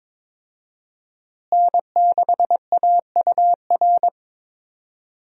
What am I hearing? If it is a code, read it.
Morse code, heard as a keyed tone, about 22 wpm, 700 Hz. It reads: N6AUR
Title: USA Call Signs - 5 Characters - Repeated 3x - 22wpm